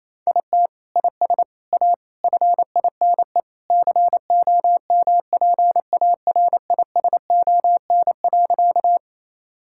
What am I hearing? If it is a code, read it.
Morse code, heard as a keyed tone, about 28 wpm, 700 Hz. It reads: IT IS A FINE COMPARISON.
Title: Sentences from Top 1600 Words 28wpm